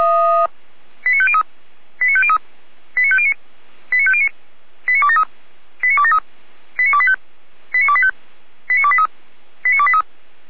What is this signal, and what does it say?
Five-tone selective call (ZVEI), heard as quick runs of five tones, tones of about 70 ms. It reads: 89503 89503 89509 89509 89273 89273 89285 89285 89283 89283